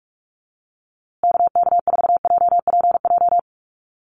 Morse code, heard as a keyed tone, about 44 wpm, 690 Hz. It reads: KK4JPJ
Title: USA Call Signs Repeated 2x 45wpm